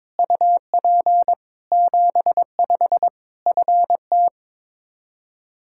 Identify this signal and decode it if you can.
Morse code, heard as a keyed tone, about 22 wpm, 700 Hz. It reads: UP 75 FT
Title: Practice QSO - Exchange 2 - Info - 22wpm